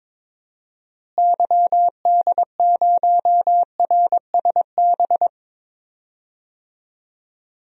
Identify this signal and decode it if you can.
Morse code, heard as a keyed tone, about 22 wpm, 700 Hz. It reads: YD0RSB
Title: DXCC-100 Call Signs - 6 Characters - Repeated 3x - 22wpm